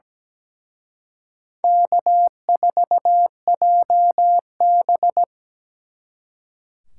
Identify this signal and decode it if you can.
Morse code, heard as a keyed tone, about 20 wpm, 700 Hz. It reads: K4JB
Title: Speed Racing - USA 1x2 Call Signs 17wpm